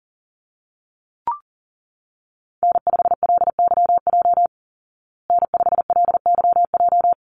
Morse code, heard as a keyed tone, about 40 wpm, 700 Hz. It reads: N5LYJ N5LYJ
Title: Call Signs Repeated 3x - 40wpm